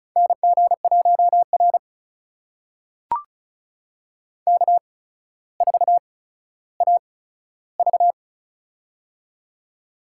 Morse code, heard as a keyed tone, about 36 wpm, spaced as for 15 wpm, 700 Hz. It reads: NG1R K4AV
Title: USA Call Signs - 4 Characters - 8x Character Spacing - 35wpm